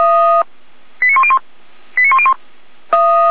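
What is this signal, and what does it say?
Five-tone selective call (ZVEI), heard as quick runs of five tones, tones of about 70 ms. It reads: 89101 89101
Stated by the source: ZVEI five-tone selective call 89058 received by SDR, from the Feuerwehr Orlamünde audio server.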